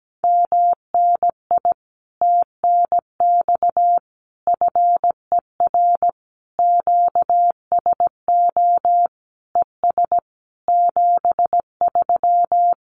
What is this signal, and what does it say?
Morse code, heard as a keyed tone, about 17 wpm, 700 Hz. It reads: MNI TNX FER QSO ES 73
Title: Common QSO Phrases Encore 17wpm